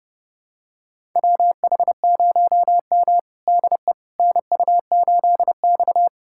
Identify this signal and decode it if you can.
Morse code, heard as a keyed tone, about 30 wpm, 700 Hz. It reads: WH0M DE NU8X
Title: USA Callsign DE Callsign - 4 Characters - 30wpm